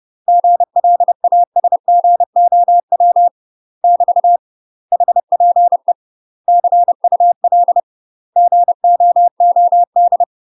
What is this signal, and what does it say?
Morse code, heard as a keyed tone, about 30 wpm, 700 Hz. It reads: GLASGOW <BT> HPE CUL GOOD
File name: UK MASTER QSOs TRAINING / 30WPM / UK MASTER QSOs 0002